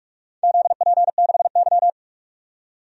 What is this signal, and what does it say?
Morse code, heard as a keyed tone, about 47 wpm, 700 Hz. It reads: ZP6Y
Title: DXCC-100 Call Signs - 4 Characters - 45wpm